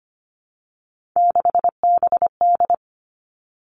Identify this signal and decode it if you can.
Morse code, heard as a keyed tone, about 25 wpm, 700 Hz. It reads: 6BD
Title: Three Challenging Letter-Number Combinations 25wpm